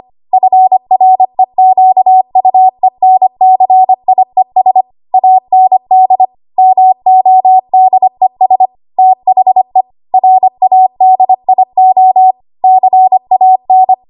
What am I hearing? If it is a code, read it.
Morse code, heard as a keyed tone, about 25 wpm, 750 Hz. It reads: FREQUENCIES AND MODES THE RADIO CAN
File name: Archive / 25 WPM / 25WPM